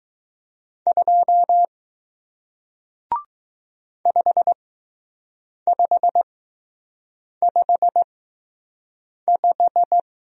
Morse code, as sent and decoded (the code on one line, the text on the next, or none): ..--- / ..... ..... ..... .....
2 5555